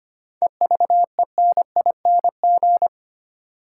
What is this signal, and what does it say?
Morse code, heard as a keyed tone, about 25 wpm, 700 Hz. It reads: EVENING
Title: Seven Letter Words - 25wpm